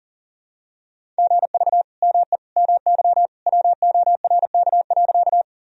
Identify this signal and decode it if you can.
Morse code, heard as a keyed tone, about 37 wpm, 700 Hz. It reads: GV ME MY WORK.
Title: Sentences from Top 200 Words and Abbreviations 40wpm